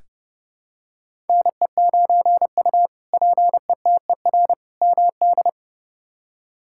Morse code, heard as a keyed tone, about 31 wpm, 700 Hz. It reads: NE9U PETER MD
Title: K1USN SST QSO Practice - 30wpm